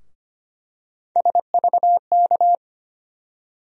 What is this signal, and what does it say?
Morse code, heard as a keyed tone, about 25 wpm, 700 Hz. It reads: SVK
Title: USA Call Sign Postfixes - 3 Letters 25wpm